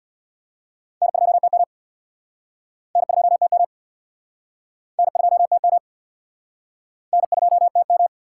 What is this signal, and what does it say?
Morse code, heard as a keyed tone, about 64 wpm, 690 Hz. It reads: N2TM N2TM N2TM N2TM